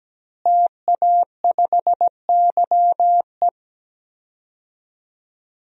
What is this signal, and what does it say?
Morse code, heard as a keyed tone, about 17 wpm, 700 Hz. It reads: TA5YE
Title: DXCC-100 Call Signs - 5 Characters - Repeated 3x - 17wpm